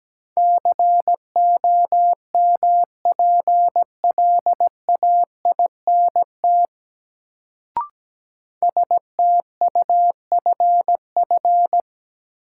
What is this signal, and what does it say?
Morse code, heard as a keyed tone, about 17 wpm, 700 Hz. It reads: COMPLAINT STUFF
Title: Sentences from Top 1400 Words - New Word Drill - 17wpm